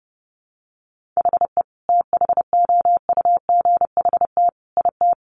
Morse code, heard as a keyed tone, about 30 wpm, 700 Hz. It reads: HE THOUGHT IT
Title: Sentences from Top 1200 Words 30wpm